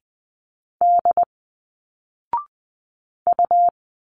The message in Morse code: D U